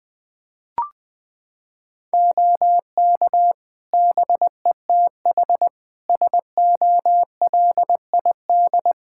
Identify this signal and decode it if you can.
Morse code, heard as a keyed tone, about 20 wpm, 700 Hz. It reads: OK BETH SOLID
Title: Practice QSO - Exchange 3 - More Info - 20wpm